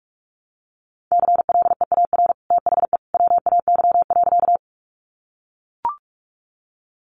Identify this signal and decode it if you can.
Morse code, heard as a keyed tone, about 44 wpm, 700 Hz. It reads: CLEAR THE WAY.
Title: Sentences from Top 500 Words 45wpm